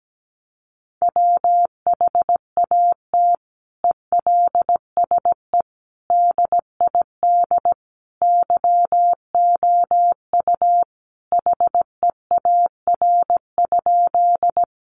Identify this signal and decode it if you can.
Morse code, heard as a keyed tone, about 17 wpm, 700 Hz. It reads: WHAT ELSE DID YOU HEAR?
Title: Sentences from Top 800 Words 17wpm